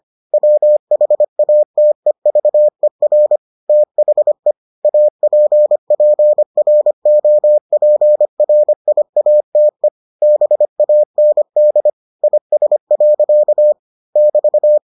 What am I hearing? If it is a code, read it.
Morse code, heard as a keyed tone, about 25 wpm, 600 Hz. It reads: WHATEVER THE APPROPRIATE BAND IS. <BT>